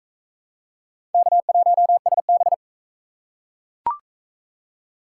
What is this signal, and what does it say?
Morse code, heard as a keyed tone, about 40 wpm, 700 Hz. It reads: K1SB